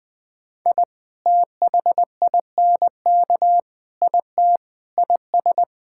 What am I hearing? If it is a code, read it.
Morse code, heard as a keyed tone, about 21 wpm, 700 Hz. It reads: I THINK IT IS
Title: Sentences with 4-Word Phrases - Spoken Beginning - 20wpm